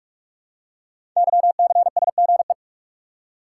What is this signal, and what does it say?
Morse code, heard as a keyed tone, about 44 wpm, 700 Hz. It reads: YCSGE
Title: Code Groups - 5 Letters - 45wpm